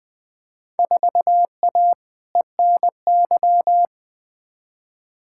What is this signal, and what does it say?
Morse code, heard as a keyed tone, about 20 wpm, 700 Hz. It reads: 4A ENY
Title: Field Day Exchanges - 20wpm